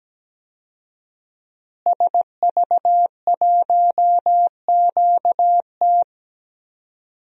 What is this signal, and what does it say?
Morse code, heard as a keyed tone, about 17 wpm, 700 Hz. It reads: SV1QT
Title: DXCC-100 Call Signs - 5 Characters - Repeated 3x - 17wpm